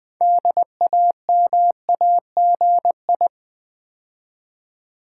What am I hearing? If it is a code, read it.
Morse code, heard as a keyed tone, about 20 wpm, 700 Hz. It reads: DAMAGI
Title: Build A Word - 3 Letters At A Time - 8-Letter Words - 20wpm